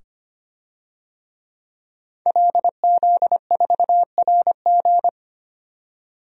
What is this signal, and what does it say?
Morse code, heard as a keyed tone, about 25 wpm, 700 Hz. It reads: LZ4RG
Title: DXCC-100 Call Signs - 5 Characters - 25wpm